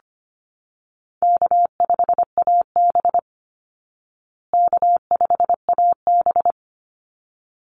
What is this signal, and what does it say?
Morse code, heard as a keyed tone, about 25 wpm, 700 Hz. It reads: K5AB K5AB
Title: Easy Call Signs Repeated 2x - 25wpm